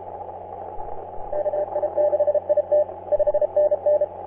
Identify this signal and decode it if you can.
Morse code, heard as a keyed tone, about 30 wpm, 600 Hz. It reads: KI6IT 5NN
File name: CN8KD on 10m CW